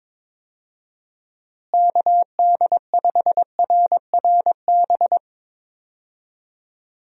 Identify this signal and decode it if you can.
Morse code, heard as a keyed tone, about 22 wpm, 700 Hz. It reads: KD5RRB
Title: USA Call Signs - 6 Characters - Repeated 2x - 22wpm